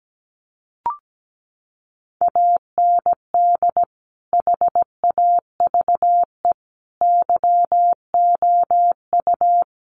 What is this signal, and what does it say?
Morse code, heard as a keyed tone, about 17 wpm, 700 Hz. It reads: AND HAVE YOU
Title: Sentences from Top 200 Words 17wpm